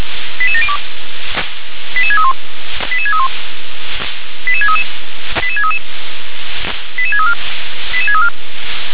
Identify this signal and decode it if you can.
Five-tone selective call (ZVEI), heard as quick runs of five tones, tones of about 80 ms. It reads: 88702 88631 88631 88633 88633 88634 88634